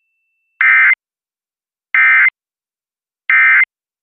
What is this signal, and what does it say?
Emergency Alert System (SAME) tones, heard as harsh buzzing bursts. It is heard from 0.6 to 3.6 s.